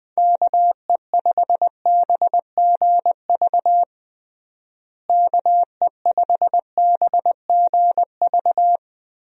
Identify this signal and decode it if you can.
Morse code, heard as a keyed tone, about 20 wpm, 700 Hz. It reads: KE5BGV KE5BGV
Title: USA Call Signs - 6 Characters - Repeated 3x - 20wpm